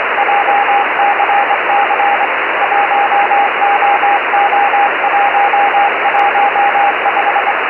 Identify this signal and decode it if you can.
Morse code, heard as a keyed tone, about 26 wpm, 810 Hz, with radio noise. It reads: JCC 1OO115
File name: JP1LRT-430CW